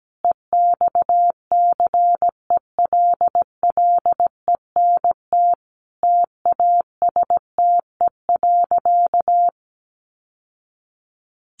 Morse code, heard as a keyed tone, about 17 wpm, 700 Hz. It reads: EXCELLENT TASTE.
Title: Sentences from Top 1100 Words 17wpm